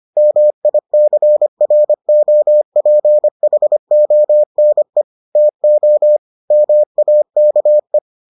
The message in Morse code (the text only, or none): MICROPHONE TO MAKE